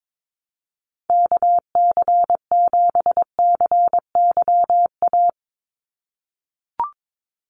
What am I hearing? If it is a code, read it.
Morse code, heard as a keyed tone, about 22 wpm, 700 Hz. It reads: KC7CYA
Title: USA Call Signs 22wpm